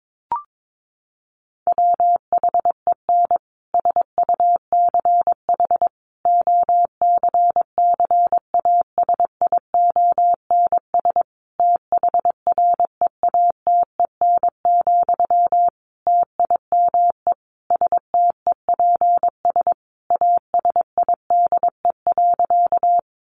Morse code, heard as a keyed tone, about 22 wpm, 700 Hz. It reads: WHEN SUCH OCCASIONS THREATEN, TIME STEPS ASIDE.
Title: A Sound Of Thunder 22wpm